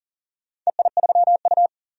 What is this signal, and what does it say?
Morse code, heard as a keyed tone, about 40 wpm, 700 Hz. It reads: EI3U
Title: DXCC-100 Call Signs - 4 Characters - 40wpm